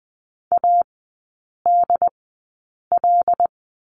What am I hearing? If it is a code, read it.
Morse code, heard as a keyed tone, about 20 wpm, spaced as for 8 wpm, 690 Hz. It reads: ADL
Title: 093 Lesson 16e - Letter c - Three Characters 20wpm